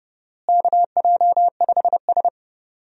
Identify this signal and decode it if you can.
Morse code, heard as a keyed tone, about 30 wpm, 700 Hz. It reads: KJ5S
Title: USA Call Signs - 4 Characters - Repeated 2x - 30wpm